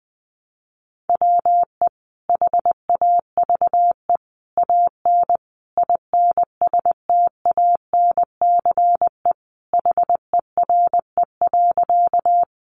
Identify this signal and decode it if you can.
Morse code, heard as a keyed tone, about 19 wpm, 700 Hz. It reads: WE HAVE AN INSTANCE HERE.
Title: Sentences from Top 800 Words 20wpm